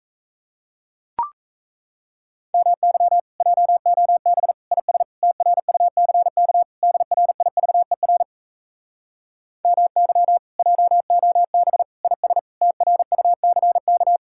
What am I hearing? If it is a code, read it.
Morse code, heard as a keyed tone, about 38 wpm, 700 Hz. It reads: MY JOB IS TRUCK DRIVER MY JOB IS TRUCK